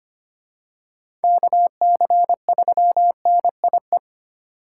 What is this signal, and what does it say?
Morse code, heard as a keyed tone, about 25 wpm, 700 Hz. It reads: KC3NIE
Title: K1USN SST QSO Practice - 25wpm